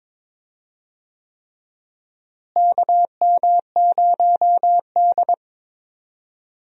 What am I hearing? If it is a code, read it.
Morse code, heard as a keyed tone, about 22 wpm, 700 Hz. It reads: KM0D
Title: USA Call Signs - 4 Characters - Repeated 3x - 22wpm